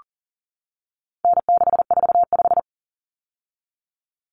Morse code, heard as a keyed tone, about 40 wpm, 700 Hz. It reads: N645
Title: Common Cut Numbers - 4-Digit 40wpm